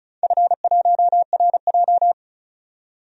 Morse code, heard as a keyed tone, about 36 wpm, 700 Hz. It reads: F1RJ
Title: DXCC-100 Call Signs - 4 Characters - 35wpm